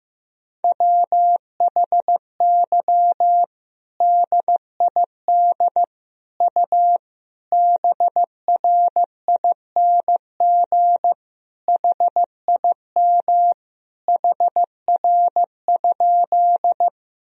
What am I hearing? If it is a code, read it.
Morse code, heard as a keyed tone, about 15 wpm, 700 Hz. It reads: WHY DID U BRING HIM HR?
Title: Sentences from Top 500 Words and Abbreviations 15wpm